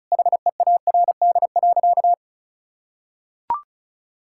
Morse code, heard as a keyed tone, about 34 wpm, 700 Hz. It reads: HEARD.
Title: Sentences from Top 1000 Words - Review Part 5 - 35wpm